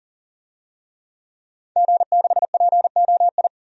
Morse code, heard as a keyed tone, about 40 wpm, 700 Hz. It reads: G6POI
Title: DXCC-100 Call Signs - 5 Characters - 40wpm